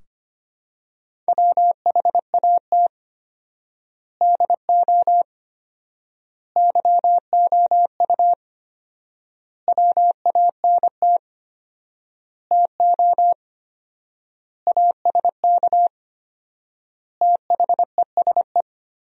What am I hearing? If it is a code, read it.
Morse code, heard as a keyed tone, about 25 wpm, 700 Hz. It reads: WHAT DO YOU WANT TO ASK THESE
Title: 4x Word-Spacing - Sentences from Top 200 Words - 25wpm